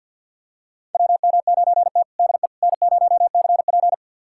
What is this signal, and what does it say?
Morse code, heard as a keyed tone, about 52 wpm, 700 Hz. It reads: WM9T DE N0CP
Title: USA Callsign DE Callsign - 4 Characters - 50wpm